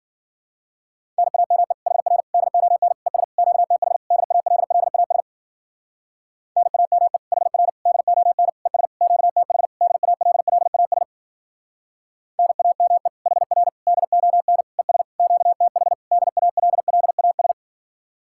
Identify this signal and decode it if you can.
Morse code, heard as a keyed tone, about 52 wpm, 700 Hz. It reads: NAME HR DON ES QTH DALLAS NAME HR DON ES QTH DALLAS NAME HR DON ES QTH DALLAS